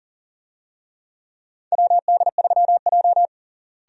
Morse code, heard as a keyed tone, about 40 wpm, 700 Hz. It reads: WD3J
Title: USA Call Signs - 4 Characters - Repeated 2x - 40wpm